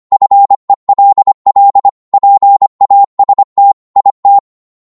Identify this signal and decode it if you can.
Morse code, heard as a keyed tone, about 25 wpm, 800 Hz. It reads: FELL PAST IT